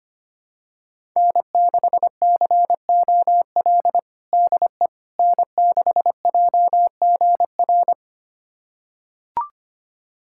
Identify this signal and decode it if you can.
Morse code, heard as a keyed tone, about 25 wpm, 700 Hz. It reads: N6COL DE N6JGR
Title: USA Callsign DE Callsign - 5 Characters - 25wpm